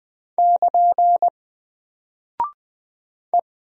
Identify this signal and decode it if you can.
Morse code, heard as a keyed tone, about 20 wpm, 700 Hz. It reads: <KN> E
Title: Single Character (Letter:Number:Punctuation:Prosign) Alt 20wpm